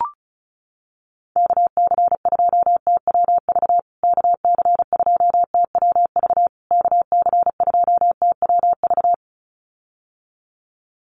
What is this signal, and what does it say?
Morse code, heard as a keyed tone, about 34 wpm, 700 Hz. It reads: KC2TWV KC2TWV KC2TWV
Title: USA Call Signs Repeated 3x 35wpm